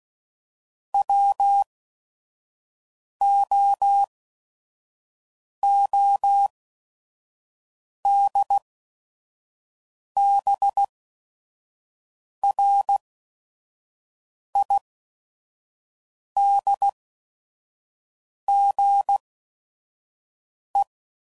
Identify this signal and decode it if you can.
Morse code, heard as a keyed tone, about 16 wpm, spaced as for 5 wpm, 800 Hz. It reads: WOODBRIDGE
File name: CW Code Groups @5WPM / fromSM's / Cities